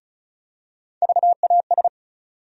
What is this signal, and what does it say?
Morse code, heard as a keyed tone, about 36 wpm, 700 Hz. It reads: VAS
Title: USA Call Sign Postfixes - 3 Letters 35wpm